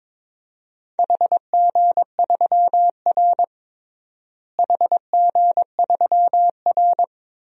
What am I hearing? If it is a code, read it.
Morse code, heard as a keyed tone, about 22 wpm, 700 Hz. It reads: HG3R HG3R
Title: DXCC-100 Call Signs - 4 Characters - Repeated 2x - 22wpm